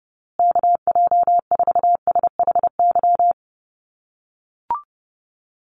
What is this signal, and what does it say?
Morse code, heard as a keyed tone, about 30 wpm, 700 Hz. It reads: KJ4SHY
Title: USA Call Signs Repeated 3x 30wpm